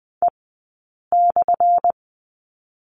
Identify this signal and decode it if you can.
Morse code, heard as a keyed tone, about 19 wpm, spaced as for 8 wpm, 700 Hz. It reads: E/
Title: 177 Lesson 29d - Slash - Two Characters 20wpm